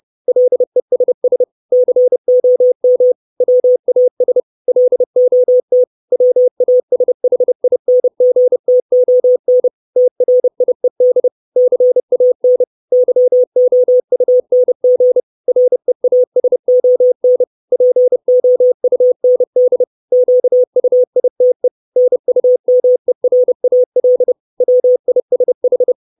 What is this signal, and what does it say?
Morse code, heard as a keyed tone, about 30 wpm, 500 Hz. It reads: LESS COM WAS LOT WASHINGTON TRIED CAN YOUNG REASON POUND QUITE NUMERAL WISH
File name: 632 Words 30WPM